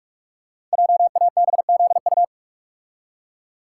Morse code, heard as a keyed tone, about 44 wpm, 700 Hz. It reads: JABZU
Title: Code Groups - 5 Letters - 45wpm